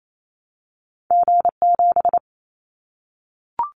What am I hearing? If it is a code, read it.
Morse code, heard as a keyed tone, about 28 wpm, 700 Hz. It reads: G7
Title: Callsign Prefixes for Top 25 DXCC - Repeated 3x - 28wpm